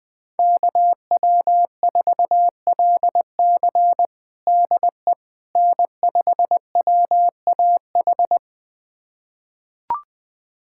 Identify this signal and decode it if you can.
Morse code, heard as a keyed tone, about 20 wpm, 700 Hz. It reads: KW4LC DE N5WAH
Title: USA Callsign DE Callsign - 5 Characters - 20wpm